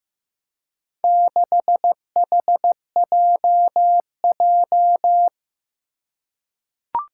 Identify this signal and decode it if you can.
Morse code, heard as a keyed tone, about 15 wpm, 700 Hz. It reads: 6HJJ
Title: Four Challenging Letter-Number Combinations 15wpm